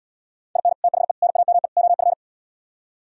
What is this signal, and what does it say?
Morse code, heard as a keyed tone, about 77 wpm, 700 Hz. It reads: IN THE NAME OF